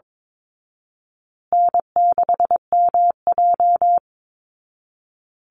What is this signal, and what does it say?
Morse code, heard as a keyed tone, about 22 wpm, 700 Hz. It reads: N6MJ
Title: Easy Call Signs Repeated 3x - 22wpm